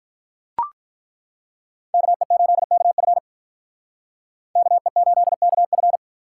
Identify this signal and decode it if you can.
Morse code, heard as a keyed tone, about 43 wpm, 700 Hz. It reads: KE8KF KE8KF
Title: Speed Racing - USA 2x2 Call Signs 35wpm